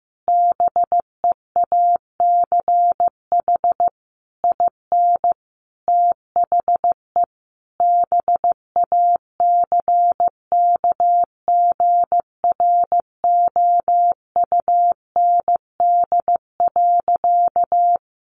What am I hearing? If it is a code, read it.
Morse code, heard as a keyed tone, about 15 wpm, 700 Hz. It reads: BEACH IN THE BACKGROUND.
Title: Sentences from Top 1500 Words 15wpm